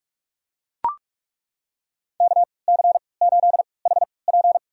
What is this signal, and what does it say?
Morse code, heard as a keyed tone, about 41 wpm, spaced as for 25 wpm, 700 Hz. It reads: KC8HP
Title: USA Call Signs - 5 Characters - 3x Character Spacing - 45wpm